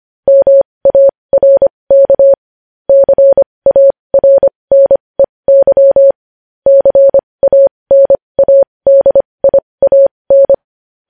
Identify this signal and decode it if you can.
Morse code, heard as a keyed tone, about 25 wpm, spaced as for 20 wpm, 550 Hz. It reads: MARK CARNEY CANADIAN
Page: news-headlines-in-morse-code-at-20-wpm